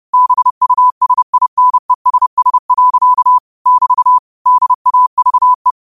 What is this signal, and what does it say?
Morse code, heard as a keyed tone, about 31 wpm, 1000 Hz. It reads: BUSINESS. <BT> DAVE
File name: qotdfiles / 30wpm